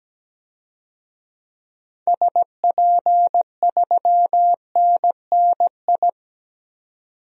Morse code, heard as a keyed tone, about 17 wpm, 700 Hz. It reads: SP3NNI
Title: DXCC-100 Call Signs - 6 Characters - 17wpm